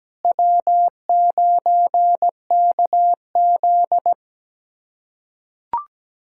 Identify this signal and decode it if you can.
Morse code, heard as a keyed tone, about 17 wpm, 700 Hz. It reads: W9KZ
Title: USA Callsign DE Callsign - 4 Characters - 17wpm